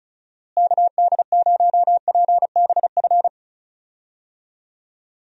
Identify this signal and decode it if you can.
Morse code, heard as a keyed tone, about 34 wpm, 700 Hz. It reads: KD0PBF
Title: USA Call Signs - 6 Characters - Repeated 3x - 35wpm